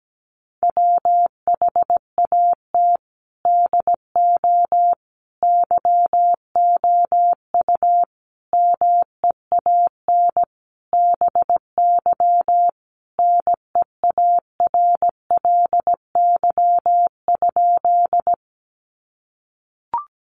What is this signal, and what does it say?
Morse code, heard as a keyed tone, about 17 wpm, 700 Hz. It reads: WHAT DO YOU MEAN BY NEARLY?
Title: Sentences from Top 1300 Words 17wpm